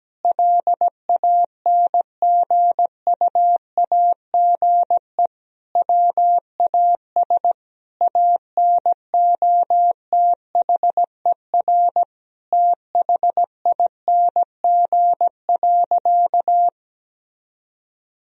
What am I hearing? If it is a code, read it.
Morse code, heard as a keyed tone, about 17 wpm, 700 Hz. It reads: LANGUAGE WAS ANOTHER THING.
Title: Sentences from Top 1000 Words - Review Part 3 - 17wpm